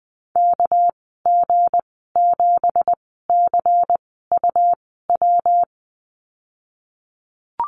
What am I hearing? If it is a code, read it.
Morse code, heard as a keyed tone, about 20 wpm, spaced as for 15 wpm, 700 Hz. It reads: KG7CUW
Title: 231 Lesson 37g - Letter x - Call Signs 20wpm